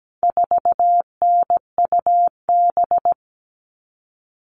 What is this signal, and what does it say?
Morse code, heard as a keyed tone, about 17 wpm, 700 Hz. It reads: 4NUB
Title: All Cut Numbers - 4-Digit 17wpm